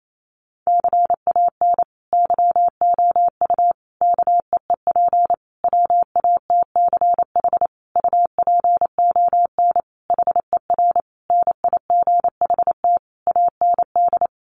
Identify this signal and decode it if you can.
Morse code, heard as a keyed tone, about 28 wpm, 700 Hz. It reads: CAN YOU KEEP WATCH UPON HER NIGHT AND
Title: Sentences from Top 1100 Words 28wpm